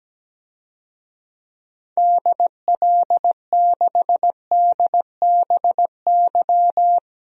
Morse code, heard as a keyed tone, about 17 wpm, 700 Hz. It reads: DL6DBY
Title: DXCC-100 Call Signs - 6 Characters - Repeated 3x - 17wpm